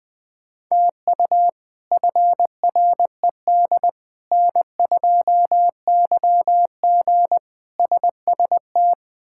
Morse code, heard as a keyed tone, about 20 wpm, 700 Hz. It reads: TU FRED N2YG SST
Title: K1USN SST QSO Practice - 20wpm